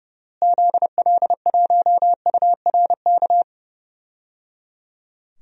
Morse code, heard as a keyed tone, about 30 wpm, 700 Hz. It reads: ZL1URK
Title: DXCC-100 Call Signs - 6 Characters - Repeated 3x - 30wpm